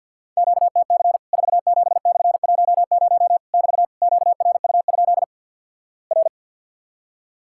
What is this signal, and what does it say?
Morse code, heard as a keyed tone, about 50 wpm, 700 Hz. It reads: QTC 47/10 <BT> QRU? R
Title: Worked All Europe DX Contest - QTC Exchange with Repeats - 50wpm